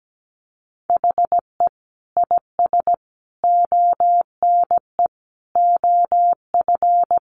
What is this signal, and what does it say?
Morse code, heard as a keyed tone, about 17 wpm, 700 Hz. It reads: HE IS ONE OF
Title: Sets of 5 Words Encore 17wpm